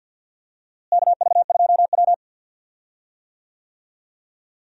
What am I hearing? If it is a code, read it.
Morse code, heard as a keyed tone, about 46 wpm, 700 Hz. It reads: KV2W
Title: USA Call Signs - 4 Characters - Repeated 2x - 50wpm